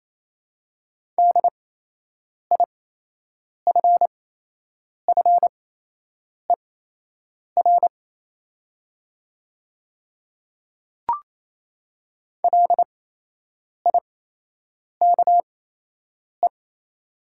Morse code, heard as a keyed tone, about 28 wpm, spaced as for 8 wpm, 700 Hz. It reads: DIFFER LIKE
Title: 8x Character Spacing - Top 300 Words - 28wpm